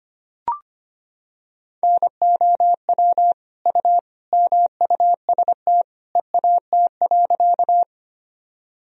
Morse code, heard as a keyed tone, about 25 wpm, 700 Hz. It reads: NOW U MUST EAT.
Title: Sentences from Top 300 Words and Abbreviations 25wpm